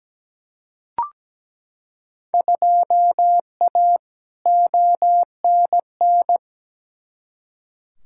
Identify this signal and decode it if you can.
Morse code, heard as a keyed tone, about 17 wpm, 700 Hz. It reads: 2A ONN
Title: Field Day Exchanges Encore - 17wpm